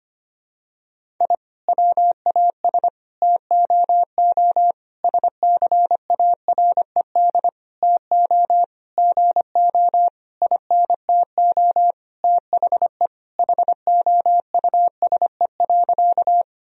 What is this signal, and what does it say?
Morse code, heard as a keyed tone, about 25 wpm, 700 Hz. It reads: I WAS TOO SCARED TO GO INTO THE HOUSE.
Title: Sentences from Top 1000 Words - Review Part 4 - 25wpm